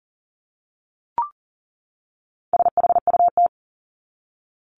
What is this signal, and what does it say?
Morse code, heard as a keyed tone, about 39 wpm, 700 Hz. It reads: SHUT
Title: Top 1751-2000 Words 40wpm